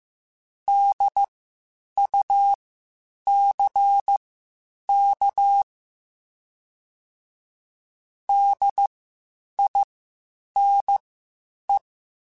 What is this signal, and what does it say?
Morse code, heard as a keyed tone, about 15 wpm, spaced as for 8 wpm, 800 Hz. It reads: DUCK DINE
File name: Spaced Alphas / Spaced Alphas mp3s / Ds sp @13